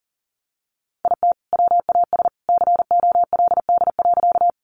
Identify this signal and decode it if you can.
Morse code, heard as a keyed tone, about 40 wpm, 700 Hz. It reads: IT WAS COLD.
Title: Sentences from Top 400 Words 40wpm